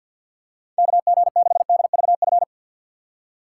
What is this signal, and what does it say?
Morse code, heard as a keyed tone, about 52 wpm, 700 Hz. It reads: KG6DVF
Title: USA Call Signs - 6 Characters - 6x Character Spacing - 50wpm